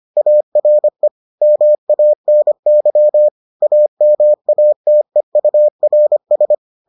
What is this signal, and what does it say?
Morse code, heard as a keyed tone, about 25 wpm, 600 Hz. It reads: ARE MANY AMATEURS